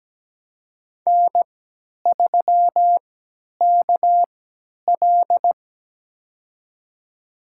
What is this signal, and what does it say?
Morse code, heard as a keyed tone, about 17 wpm, spaced as for 10 wpm, 700 Hz. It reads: N3KL
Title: USA Call Signs - 4 Characters - 3x Character Spacing - 17wpm